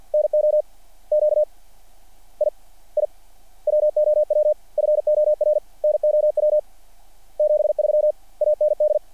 Morse code, heard as a keyed tone, about 48 wpm, 600 Hz. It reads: NO Q I I WOW FOR NOW 73 AND
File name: archived / CRSnet